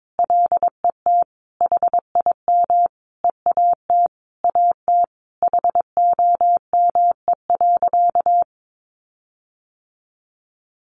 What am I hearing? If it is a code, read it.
Morse code, heard as a keyed tone, about 22 wpm, 700 Hz. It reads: LET HIM EAT AT HOME.
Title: Sentences from Top 1400 Words 22wpm